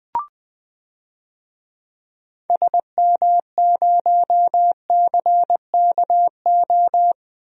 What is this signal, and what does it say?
Morse code, heard as a keyed tone, about 20 wpm, 700 Hz. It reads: SM0CKO
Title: DXCC-100 Call Signs - 6 Characters - 20wpm